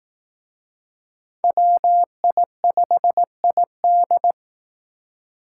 Morse code, heard as a keyed tone, about 19 wpm, 700 Hz. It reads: WI5ID